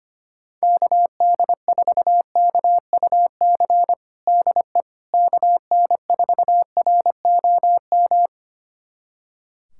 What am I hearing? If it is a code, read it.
Morse code, heard as a keyed tone, about 25 wpm, 700 Hz. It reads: KD4KUC DE KN4ROM
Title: USA Callsign DE Callsign 25wpm